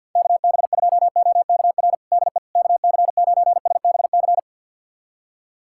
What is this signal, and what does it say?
Morse code, heard as a keyed tone, about 51 wpm, 700 Hz. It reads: KB2OKR DE KC9SBC
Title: USA Callsign DE Callsign - 6 Characters - 50wpm